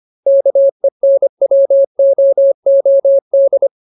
Morse code, heard as a keyed tone, about 25 wpm, 550 Hz. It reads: KENWOOD
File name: Ultimate Word List / 25WPM Ultimate